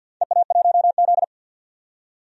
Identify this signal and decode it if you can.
Morse code, heard as a keyed tone, about 52 wpm, 700 Hz. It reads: EA1Z
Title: DXCC-100 Call Signs - 4 Characters - 50wpm